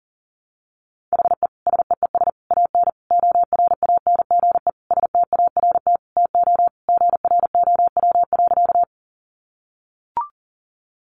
Morse code, heard as a keyed tone, about 40 wpm, 700 Hz. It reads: HE SEES AN ORANGE START TO GROW.
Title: Easy Phrases 40wpm